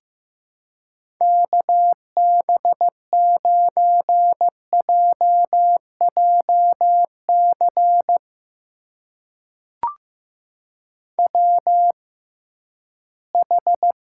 Morse code, heard as a keyed tone, about 15 wpm, spaced as for 11 wpm, 700 Hz. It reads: KB9JJC WH
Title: USA Call Signs - 6x Character Spacing - 15wpm